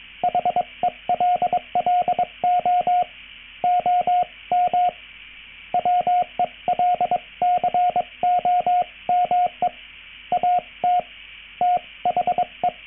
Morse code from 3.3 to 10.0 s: OM WELCOME